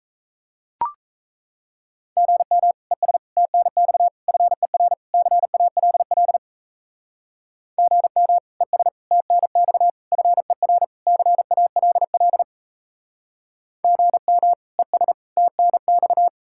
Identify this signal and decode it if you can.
Morse code, heard as a keyed tone, about 33 wpm, 700 Hz. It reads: GM ES TNX FER CALL GM ES TNX FER CALL GM ES TNX
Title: Speed-Racing Common QSO Phrases - 28wpm